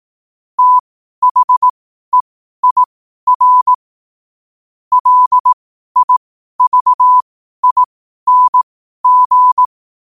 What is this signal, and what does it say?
Morse code, heard as a keyed tone, about 18 wpm, spaced as for 12 wpm, 1000 Hz. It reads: THEIR LIVING
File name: qotdfiles / 12wpm